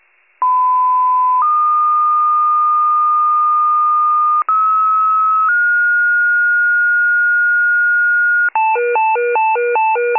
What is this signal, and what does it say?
Two-tone page, heard as a one-second tone, then a three-second tone, about 1010 then 1220 Hz, about 1290 then 1430 Hz, with noise. It is heard from 0.4 to 8.5 s.